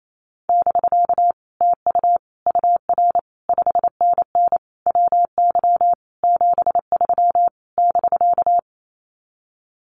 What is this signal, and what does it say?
Morse code, heard as a keyed tone, about 28 wpm, 700 Hz. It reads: <BK> TU UR 5NN WY 73 <BK>
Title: Common QSO Phrases Encore 28wpm